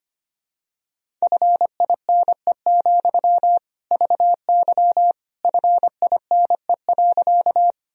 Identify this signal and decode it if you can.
Morse code, heard as a keyed tone, about 25 wpm, 700 Hz. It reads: FINE, VY FINE.
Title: Sentences from Top 500 Words and Abbreviations 25wpm